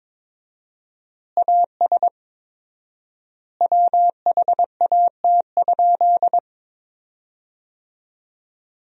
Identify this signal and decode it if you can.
Morse code, heard as a keyed tone, about 22 wpm, 700 Hz. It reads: AS WHAT?
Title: 4x Word-Spacing - Sentences from Top 200 Words - 22wpm